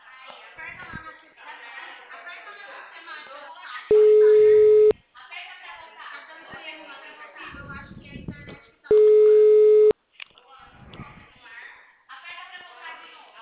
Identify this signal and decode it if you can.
Ringback tone (Europe, 425 Hz), with speech.